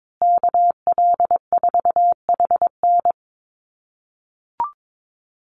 Morse code, heard as a keyed tone, about 22 wpm, 700 Hz. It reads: KL4HN
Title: USA Call Signs Repeated 3x 22wpm